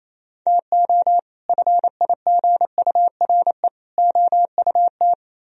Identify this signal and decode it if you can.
Morse code, heard as a keyed tone, about 36 wpm, 700 Hz. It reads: TO FIGURE OUT